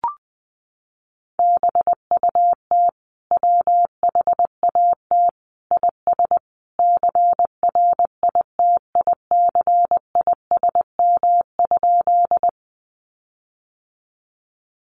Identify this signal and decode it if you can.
Morse code, heard as a keyed tone, about 20 wpm, 700 Hz. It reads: BUT WHAT IS CRITICISM?
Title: Sentences from Top 1300 Words 20wpm